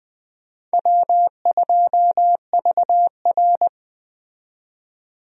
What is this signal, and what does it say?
Morse code, heard as a keyed tone, about 20 wpm, 700 Hz. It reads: W2VR
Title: USA Call Signs - 4 Characters - 20wpm